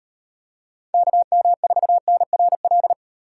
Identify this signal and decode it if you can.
Morse code, heard as a keyed tone, about 38 wpm, 700 Hz. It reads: KM4NRL